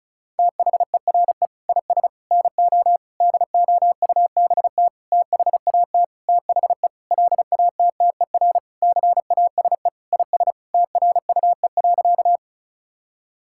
Morse code, heard as a keyed tone, about 34 wpm, 700 Hz. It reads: THERE IS NO DOUBT THAT THE LATTER CASE IS TRUE.
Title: Sentences from Top 1600 Words 35wpm